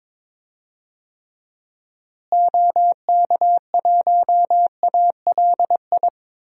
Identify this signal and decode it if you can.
Morse code, heard as a keyed tone, about 22 wpm, 700 Hz. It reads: OK1ALI
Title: DXCC-100 Call Signs - 6 Characters - Repeated 3x - 22wpm